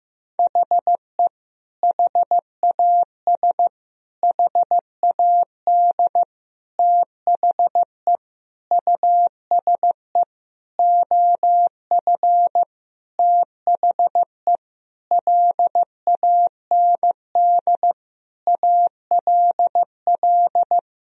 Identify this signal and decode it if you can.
Morse code, heard as a keyed tone, about 15 wpm, 700 Hz. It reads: HE HAS HAD THE USE OF THE LAND ALL
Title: Sentences from Top 1000 Words - Review Part 1 - 15wpm